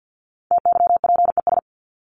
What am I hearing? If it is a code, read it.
Morse code, heard as a keyed tone, about 50 wpm, 700 Hz. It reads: TYPES
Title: ES:S Suffix - Top 100 Words with Base Words 50wpm